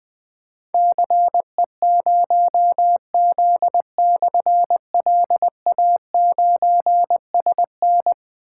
Morse code, heard as a keyed tone, about 21 wpm, 700 Hz. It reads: CE0Z/LA9SN
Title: Difficult Call Signs 20wpm